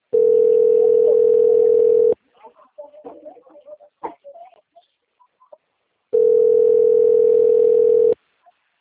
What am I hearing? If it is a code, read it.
Ringback tone (North America), 440 + 480 Hz.